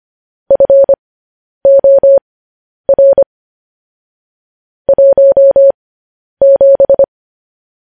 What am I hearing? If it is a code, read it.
Morse code, heard as a keyed tone, about 25 wpm, spaced as for 10 wpm, 550 Hz. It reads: FOR 17
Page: News Headlines in Morse Code at 10 WPM